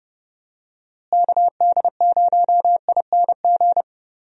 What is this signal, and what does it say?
Morse code, heard as a keyed tone, about 30 wpm, 700 Hz. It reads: KD0ING
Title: USA Call Signs - 6 Characters - Repeated 2x - 30wpm